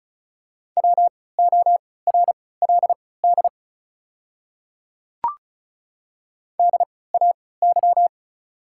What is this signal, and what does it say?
Morse code, heard as a keyed tone, about 35 wpm, spaced as for 20 wpm, 700 Hz. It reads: WORLD DAY
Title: 3x Character Spacing - Top 200 Words - 35wpm